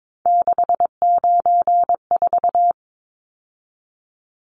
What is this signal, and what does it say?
Morse code, heard as a keyed tone, about 22 wpm, 700 Hz. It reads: 694
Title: Numbers - 3-Digit 22wpm